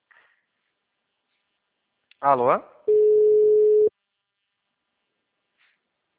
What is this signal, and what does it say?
Ringback tone (Europe, 425 Hz), with speech.